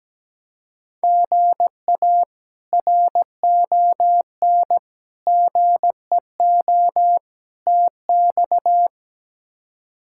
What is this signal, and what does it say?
Morse code, heard as a keyed tone, about 17 wpm, 700 Hz. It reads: GA RON GEO TX
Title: K1USN SST QSO Practice - 17wpm